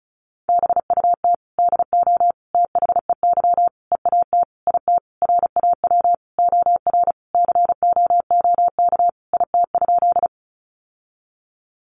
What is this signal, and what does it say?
Morse code, heard as a keyed tone, about 34 wpm, 700 Hz. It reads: BUT DO THEY EAT IT RAW OR COOK IT?
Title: Sentences from Top 1500 Words 35wpm